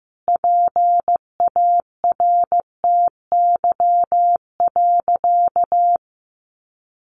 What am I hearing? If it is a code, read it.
Morse code, heard as a keyed tone, about 15 wpm, 700 Hz. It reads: PARTY.
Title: Sentences from Top 1500 Words 15wpm